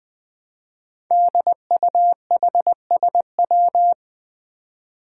Morse code, heard as a keyed tone, about 20 wpm, 700 Hz. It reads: DUHSW
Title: Five Challenging Letter-Number Combinations 20wpm